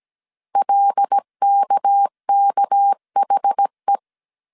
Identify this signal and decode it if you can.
Morse code, heard as a keyed tone, about 17 wpm, 800 Hz. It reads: LKKHE
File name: CW / DET / 1 6 1500C S A 16wpm 2635